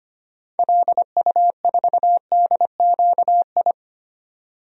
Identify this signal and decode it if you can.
Morse code, heard as a keyed tone, about 25 wpm, 700 Hz. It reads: LU4DQI
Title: DXCC-100 Call Signs - 6 Characters - 25wpm